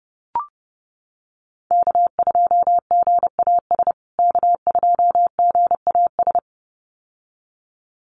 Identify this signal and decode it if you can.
Morse code, heard as a keyed tone, about 30 wpm, 700 Hz. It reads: K2GAS K2GAS
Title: USA Call Signs Repeated 2x 30wpm